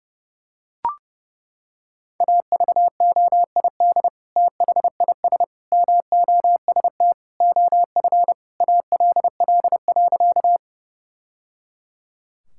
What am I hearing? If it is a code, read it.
Morse code, heard as a keyed tone, about 30 wpm, 700 Hz. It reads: AVOID THIS MOST OF ALL.
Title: Sentences from Top 1000 Words - Review Part 6 - 30wpm